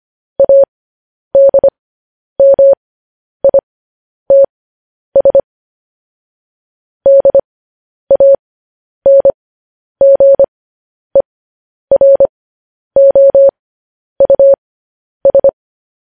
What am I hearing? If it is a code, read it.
Morse code, heard as a keyed tone, about 25 wpm, spaced as for 10 wpm, 550 Hz. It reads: ADMITS DANGEROUS
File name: podcast / News headlines-1x-25x10